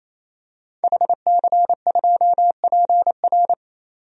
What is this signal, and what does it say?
Morse code, heard as a keyed tone, about 28 wpm, 700 Hz. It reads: HC2PR
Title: DXCC-100 Call Signs - 5 Characters - 28wpm